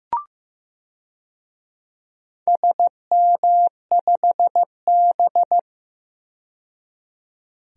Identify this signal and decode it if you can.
Morse code, heard as a keyed tone, about 15 wpm, 700 Hz. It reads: SM5B
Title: DXCC-100 Call Signs - 4 Characters - 15wpm